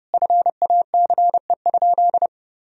Morse code, heard as a keyed tone, about 32 wpm, 700 Hz. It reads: FACE?
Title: Sentences from Top 1000 Words - Review Part 2 - 30wpm